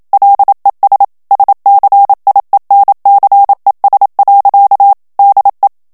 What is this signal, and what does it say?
Morse code, heard as a keyed tone, about 27 wpm, 800 Hz. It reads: LES SCIENCES. DE
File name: CW / FICHIERS-MP3 / FRANCAIS TEXTE CLAIR 1 VITESSE 26 MOTS MINUTES